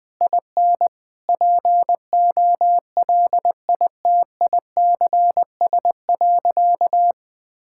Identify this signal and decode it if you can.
Morse code, heard as a keyed tone, about 20 wpm, 700 Hz. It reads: IN POLITICS.
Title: Sentences from Top 1000 Words - Review Part 6 - 20wpm